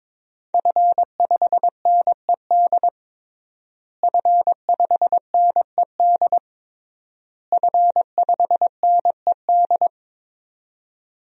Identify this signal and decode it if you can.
Morse code, heard as a keyed tone, about 22 wpm, 700 Hz. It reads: F5NED F5NED F5NED
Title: DXCC-100 Call Signs - 5 Characters - Repeated 3x - 22wpm